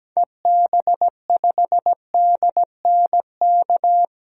Morse code, heard as a keyed tone, about 17 wpm, 700 Hz. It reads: EB5DNK
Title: DXCC-100 Call Signs - 6 Characters - 17wpm